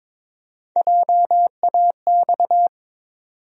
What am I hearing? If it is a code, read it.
Morse code, heard as a keyed tone, about 22 wpm, 700 Hz. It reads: JAX
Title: USA Call Sign Postfixes - 3 Letters 22wpm